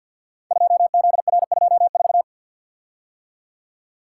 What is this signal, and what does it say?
Morse code, heard as a keyed tone, about 48 wpm, 700 Hz. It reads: 2ZR24
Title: Code Groups - 5 Letter:Numbers - 50wpm